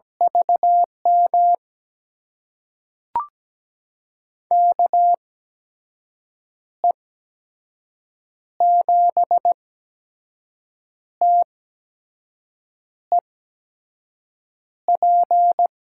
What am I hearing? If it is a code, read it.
Morse code, heard as a keyed tone, about 17 wpm, spaced as for 5 wpm, 700 Hz. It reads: VM KE7TEP
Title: USA Call Signs - 6 Characters - 8x Character Spacing - 17wpm